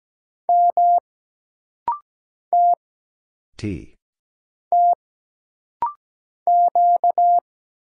Morse code, heard as a keyed tone, about 17 wpm, 700 Hz. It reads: M T T Q
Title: Single Letter-Number Alt - Rapid-Fire 17wpm